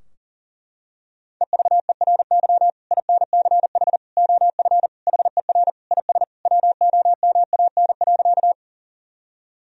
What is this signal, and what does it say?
Morse code, heard as a keyed tone, about 40 wpm, 700 Hz. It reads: EVERY INCH OF HER IS WOMAN.
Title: Sentences from Top 1000 Words - Review Part 3 - 40wpm